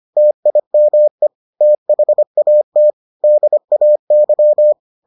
Morse code, heard as a keyed tone, about 25 wpm, 600 Hz. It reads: TIME THAT DAY